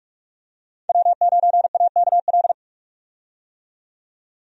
Morse code, heard as a keyed tone, about 43 wpm, 700 Hz. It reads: W9AKL
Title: USA Call Signs - 5 Characters - 45wpm